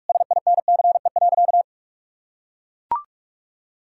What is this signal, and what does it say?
Morse code, heard as a keyed tone, about 41 wpm, 700 Hz. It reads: SINCE.
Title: Sentences from Top 1600 Words 45wpm